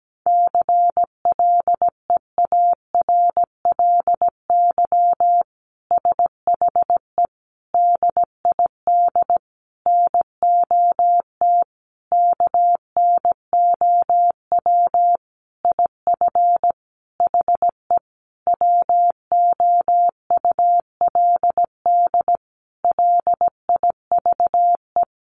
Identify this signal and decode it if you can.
Morse code, heard as a keyed tone, about 17 wpm, 700 Hz. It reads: CLEARLY SHE DID NOT KNOW IF HE WOULD LIVE
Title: Sentences from Top 1300 Words 17wpm